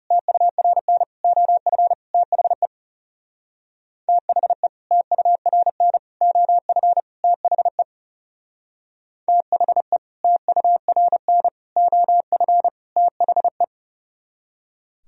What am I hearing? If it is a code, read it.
Morse code, heard as a keyed tone, about 38 wpm, 700 Hz. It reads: TURN OF THE THE TURN OF THE THE TURN OF THE